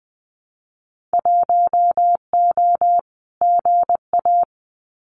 Morse code, heard as a keyed tone, about 20 wpm, 700 Hz. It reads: 1O GA
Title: Winter Field Day Exchanges 20wpm